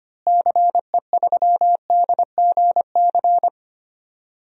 Morse code, heard as a keyed tone, about 25 wpm, 700 Hz. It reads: CE3DGC
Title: DXCC-100 Call Signs - 6 Characters - Repeated 2x - 25wpm